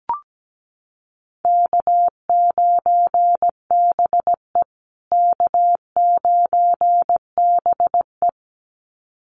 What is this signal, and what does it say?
Morse code, heard as a keyed tone, about 17 wpm, 700 Hz. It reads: K9BE K9BE
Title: USA Call Signs Repeated 2x 17wpm